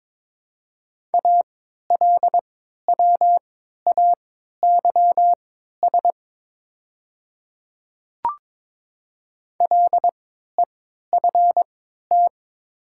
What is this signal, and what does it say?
Morse code, heard as a keyed tone, about 22 wpm, spaced as for 12 wpm, 690 Hz. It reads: ALWAYS LEFT
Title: 3x Character Spacing - Top 300 Words - 22wpm